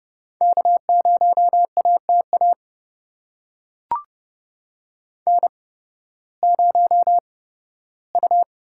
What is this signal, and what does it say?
Morse code, heard as a keyed tone, about 32 wpm, spaced as for 16 wpm, 700 Hz. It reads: K0ATA N0U
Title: USA Call Signs - 5 Characters - 8x Character Spacing - 30wpm